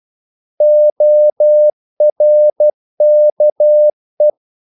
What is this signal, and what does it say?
Morse code, heard as a keyed tone, about 12 wpm, 600 Hz. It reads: ORKE
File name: Korte ord DK 12wpm